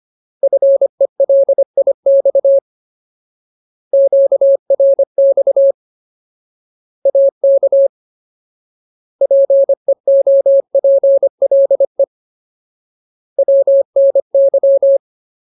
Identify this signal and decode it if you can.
Morse code, heard as a keyed tone, about 25 wpm, 550 Hz. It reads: FELIX QRX AK PEOPLE WNY
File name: Ultimate Word List / 25WPM Ultimate